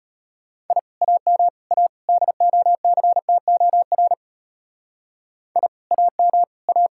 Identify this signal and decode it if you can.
Morse code, heard as a keyed tone, about 31 wpm, 700 Hz. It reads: I AM A DOCTOR I AM A
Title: Speed-Racing Common QSO Phrases Encore - 25wpm